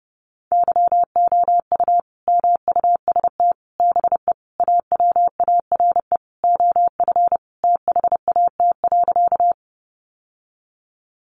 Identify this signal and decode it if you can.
Morse code, heard as a keyed tone, about 30 wpm, 700 Hz. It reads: YOU MUST BE AWARE OF THAT.
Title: Sentences from Top 1200 Words 30wpm